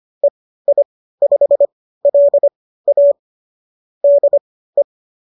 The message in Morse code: EI5LA DE

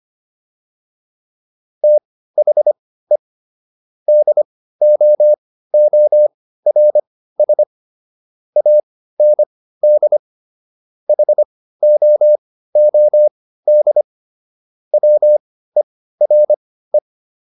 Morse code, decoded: THE DOORS AND HOOD WERE

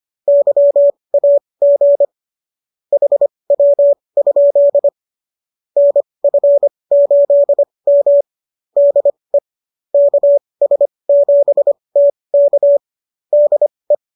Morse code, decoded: YAG HW? NF8M DE KS7T K DE

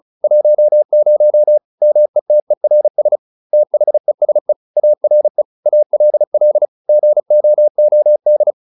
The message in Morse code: 10 METERS THESE ARE ALL GOOD

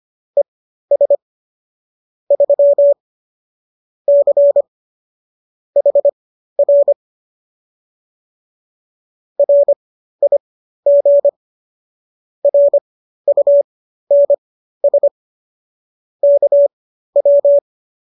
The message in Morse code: ES 3 C HR RIG RUNS KW

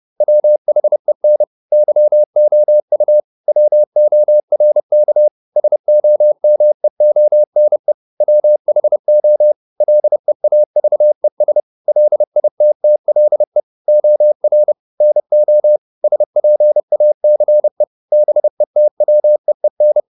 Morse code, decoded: WHEN YOU WORK SOMEONE WHO LEAVES LITTLE OR NO SPACE BETWEEN

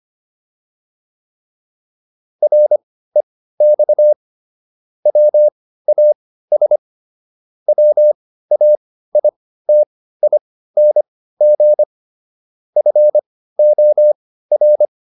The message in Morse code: REX WAS WAITING FOR

